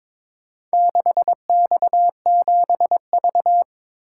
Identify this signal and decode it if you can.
Morse code, heard as a keyed tone, about 22 wpm, 700 Hz. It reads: 6X7V
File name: practice / 22wpm / Four Letter-Number Combinations - 22wpm